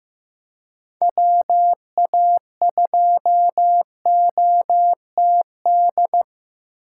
Morse code, heard as a keyed tone, about 15 wpm, 700 Hz. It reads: WA2OTD
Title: USA Call Signs - 6 Characters - 4x Character Spacing - 15wpm